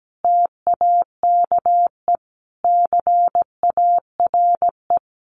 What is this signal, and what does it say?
Morse code, heard as a keyed tone, about 17 wpm, 700 Hz. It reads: TAKE CARE
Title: Easy Phrases 17wpm